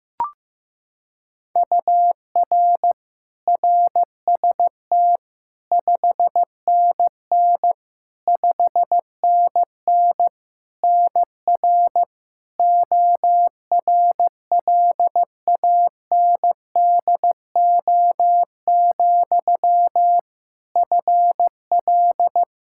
Morse code, heard as a keyed tone, about 15 wpm, 700 Hz. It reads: UR RST 5NN 5NN NR ORLANDO, FL
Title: Practice QSO - Exchange 1 - Beginning - 15wpm